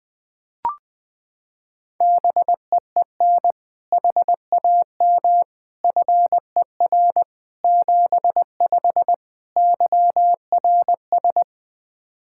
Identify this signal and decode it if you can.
Morse code, heard as a keyed tone, about 20 wpm, 700 Hz. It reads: BEEN HAM FER 75 YRS
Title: Practice QSO - Exchange 3 - More Info - 20wpm